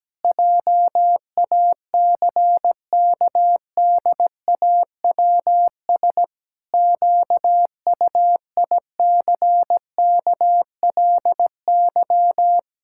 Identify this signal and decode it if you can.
Morse code, heard as a keyed tone, about 17 wpm, 700 Hz. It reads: JACKDAWS QUICKLY
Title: Pangram Challenge v2 - 17wpm